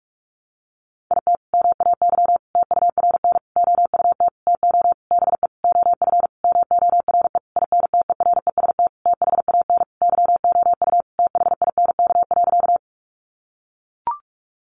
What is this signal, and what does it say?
Morse code, heard as a keyed tone, about 44 wpm, 700 Hz. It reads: IT MAY TURN OUT TO BE OF MORE INTEREST THAN YOU THINK.
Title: Sentences from Top 400 Words 45wpm